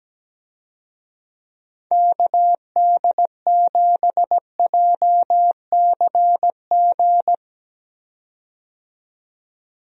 Morse code, heard as a keyed tone, about 17 wpm, 700 Hz. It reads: KD7JCG
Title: USA Call Signs - 6 Characters - 17wpm